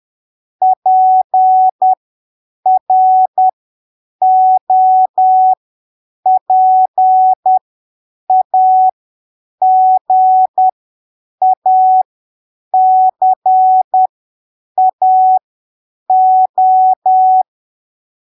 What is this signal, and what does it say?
Morse code, heard as a keyed tone, about 10 wpm, spaced as for 7 wpm, 750 Hz. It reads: PROPAGACAO
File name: arquivos / texto / texto 10 10ppm